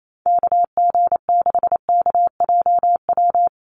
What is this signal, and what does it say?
Morse code, heard as a keyed tone, about 28 wpm, 700 Hz. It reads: KG6KJW
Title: USA Call Signs Repeated 2x 28wpm